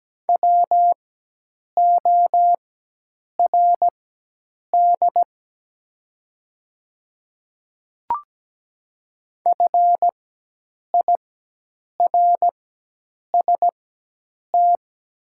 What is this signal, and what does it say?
Morse code, heard as a keyed tone, about 17 wpm, spaced as for 8 wpm, 690 Hz. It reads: WORD FIRST
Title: 4x Character Spacing - Top 100 Words - 17wpm